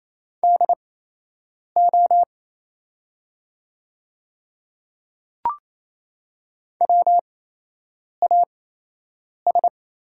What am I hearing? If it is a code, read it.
Morse code, heard as a keyed tone, about 28 wpm, spaced as for 8 wpm, 700 Hz. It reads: DO WAS
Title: 8x Character Spacing - Top 500 Words - 28wpm